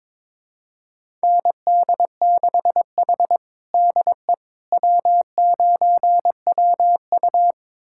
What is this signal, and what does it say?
Morse code, heard as a keyed tone, about 22 wpm, 700 Hz. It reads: ND6H DE W9WU
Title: USA Callsign DE Callsign - 4 Characters - 22wpm